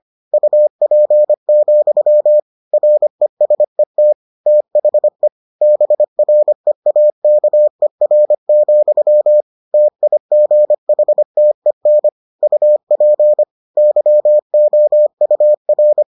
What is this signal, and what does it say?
Morse code, heard as a keyed tone, about 25 wpm, 600 Hz. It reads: UP, RESET THE BREAKER, TIGHTEN UP YOUR